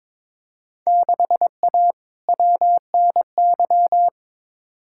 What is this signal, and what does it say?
Morse code, heard as a keyed tone, about 22 wpm, 700 Hz. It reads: 6A WNY
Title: Field Day Exchanges Encore - 22wpm